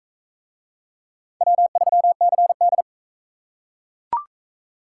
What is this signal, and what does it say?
Morse code, heard as a keyed tone, about 40 wpm, 700 Hz. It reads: W3CD